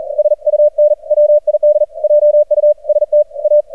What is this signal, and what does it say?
Morse code, heard as a keyed tone, about 40 wpm, 600 Hz. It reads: SUN WID JUST A
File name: archived / CRSnet